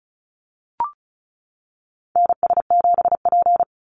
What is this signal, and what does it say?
Morse code, heard as a keyed tone, about 34 wpm, 700 Hz. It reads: NS7P
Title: NA SOTA Chaser Callsigns 35wpm